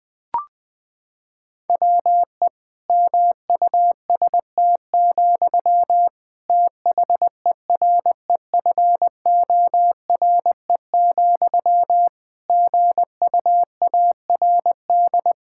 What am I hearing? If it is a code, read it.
Morse code, heard as a keyed tone, about 20 wpm, 700 Hz. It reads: WE MUST, THEREFORE, GUARD
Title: Sentences from Top 1600 Words 20wpm